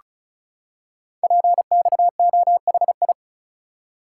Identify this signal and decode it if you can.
Morse code, heard as a keyed tone, about 36 wpm, 690 Hz. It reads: PXOHI
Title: Code Groups - 5 Letters - 35wpm